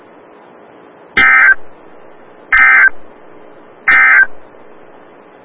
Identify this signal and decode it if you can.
Emergency Alert System (SAME) tones, heard as harsh buzzing bursts. It is heard from 1.1 to 4.4 s.
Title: Des Moines, IA NOAA Weather Radio - WXL57